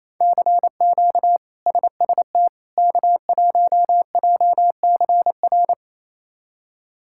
Morse code, heard as a keyed tone, about 28 wpm, 700 Hz. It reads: CQ SST K1JCR
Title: K1USN SST QSO Practice - 28wpm